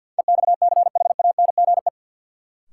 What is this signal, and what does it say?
Morse code, heard as a keyed tone, about 48 wpm, 700 Hz. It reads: EXCHANGE
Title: Sentences from Top 1000 Words - Review Part 6 - New Word Drill - 50wpm